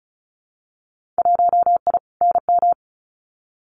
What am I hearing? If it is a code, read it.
Morse code, heard as a keyed tone, about 34 wpm, 700 Hz. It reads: 1I NM
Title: Winter Field Day Exchanges Encore 35wpm